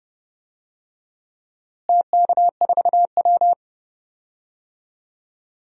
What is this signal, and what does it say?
Morse code, heard as a keyed tone, about 31 wpm, 700 Hz. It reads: TK4W
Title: DXCC-100 Call Signs - 4 Characters - Repeated 3x - 30wpm